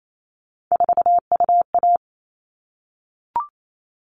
Morse code, heard as a keyed tone, about 28 wpm, 700 Hz. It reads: 4UA
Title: All Cut Numbers - 3-Digit 28wpm